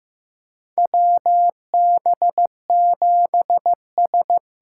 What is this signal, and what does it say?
Morse code, heard as a keyed tone, about 15 wpm, 700 Hz. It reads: WB7S
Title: USA Call Signs - 6 Characters - 8x Character Spacing - 15wpm